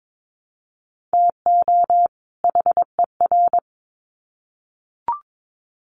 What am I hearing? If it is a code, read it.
Morse code, heard as a keyed tone, about 22 wpm, 700 Hz. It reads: TO HER
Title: Sets of 2 Words 22wpm